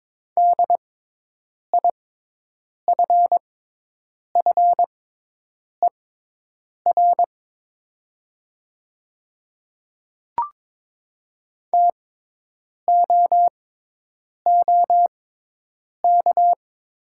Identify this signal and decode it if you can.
Morse code, heard as a keyed tone, about 22 wpm, spaced as for 8 wpm, 700 Hz. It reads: DIFFER TOOK
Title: 6x Character Spacing - Top 300 Words - 22wpm